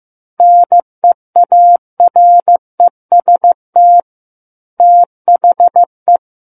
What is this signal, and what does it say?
Morse code, heard as a keyed tone, about 15 wpm, 700 Hz. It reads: NEAREST THE
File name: Chapter 6 - 15 WPM